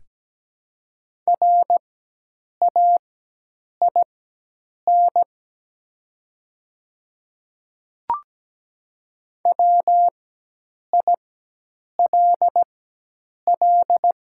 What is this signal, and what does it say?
Morse code, heard as a keyed tone, about 17 wpm, spaced as for 8 wpm, 700 Hz. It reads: RAIN WILL
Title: 4x Character Spacing - Top 300 Words - 17wpm